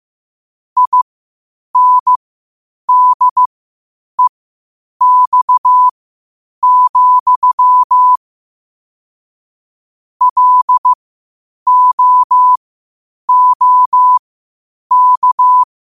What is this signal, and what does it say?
Morse code, heard as a keyed tone, about 15 wpm, spaced as for 8 wpm, 1000 Hz. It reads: INDEX, LOOK
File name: qotdfiles / 8wpm